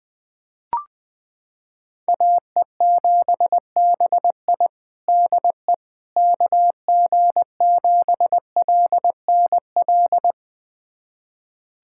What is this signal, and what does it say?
Morse code, heard as a keyed tone, about 20 wpm, 700 Hz. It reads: AE7BI DE KG7LNL
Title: USA Callsign DE Callsign 20wpm